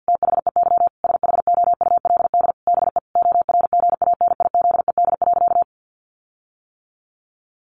Morse code, heard as a keyed tone, about 50 wpm, 700 Hz. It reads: THEY SHOULD BE ORGANIZED.
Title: Sentences from Top 1300 Words 50wpm